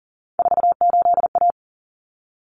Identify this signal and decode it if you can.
Morse code, heard as a keyed tone, about 40 wpm, 700 Hz. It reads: 48A
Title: Common Cut Numbers - 3-Digit 40wpm